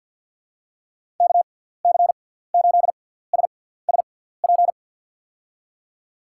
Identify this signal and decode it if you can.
Morse code, heard as a keyed tone, about 50 wpm, 700 Hz. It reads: K C 8 S S P
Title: USA Call Signs - 6 Characters - 6x Character Spacing - 50wpm